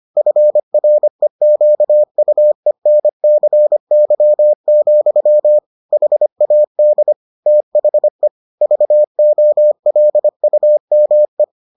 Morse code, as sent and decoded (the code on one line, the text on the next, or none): ..-. .-. --.- ..- . -. -.-. -.-- --..-- / .... .- -.. / - .... / ...- --- .-.. ..- -- .
FREQUENCY, HAD THE VOLUME